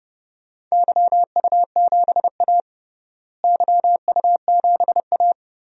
Morse code, heard as a keyed tone, about 30 wpm, 700 Hz. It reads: YU7A YU7A
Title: DXCC-100 Call Signs - 4 Characters - Repeated 3x - 30wpm